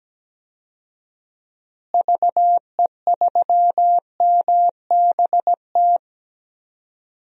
Morse code, heard as a keyed tone, about 17 wpm, 700 Hz. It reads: VE3MBT
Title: DXCC-100 Call Signs - 6 Characters - Repeated 3x - 17wpm